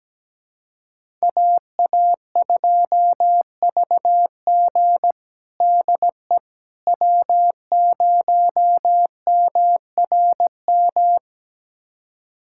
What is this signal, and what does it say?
Morse code, heard as a keyed tone, about 17 wpm, 700 Hz. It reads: AA2VG DE W0MRM
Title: USA Callsign DE Callsign - 5 Characters - 17wpm